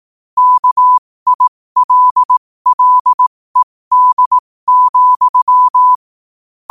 Morse code, heard as a keyed tone, about 18 wpm, spaced as for 15 wpm, 1000 Hz. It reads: KILLED,
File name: qotdfiles / 15wpm